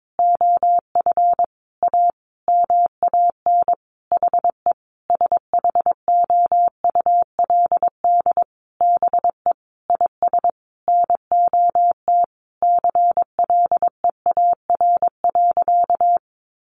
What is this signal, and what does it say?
Morse code, heard as a keyed tone, about 22 wpm, 700 Hz. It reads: OF A MAN HE SHOULD BE IS NOT CLEAR.
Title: Sentences from Top 1300 Words 22wpm